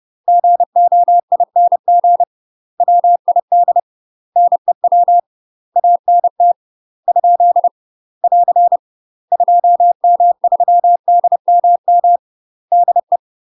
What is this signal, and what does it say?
Morse code, heard as a keyed tone, about 30 wpm, 700 Hz. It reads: GOING WID NEW ANT ? <AR> 2M3DMM DE
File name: UK MASTER QSOs TRAINING/30WPM